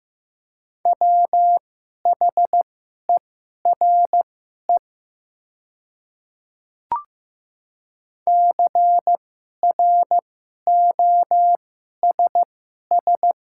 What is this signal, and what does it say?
Morse code, heard as a keyed tone, about 15 wpm, spaced as for 11 wpm, 700 Hz. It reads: WHERE CROSS
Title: 2x Character Spacing - Top 300 Words - 15wpm